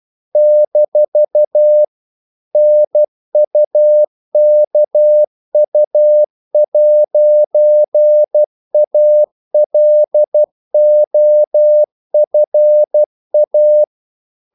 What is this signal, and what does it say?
Morse code, heard as a keyed tone, about 12 wpm, 600 Hz. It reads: - NUKU'ALOFA
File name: CW / Practice / NationalCapitals12